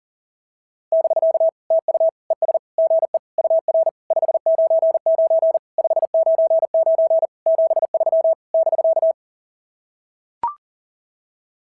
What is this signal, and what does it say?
Morse code, heard as a keyed tone, about 38 wpm, 660 Hz. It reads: <BK> TU ES GE UR 599 599 73 <BK>
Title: SOTA QSO Practice 40wpm